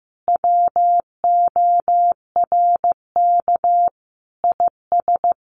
Morse code, heard as a keyed tone, about 15 wpm, 700 Hz. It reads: WORK IS
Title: Sentences from Top 600 Words 15wpm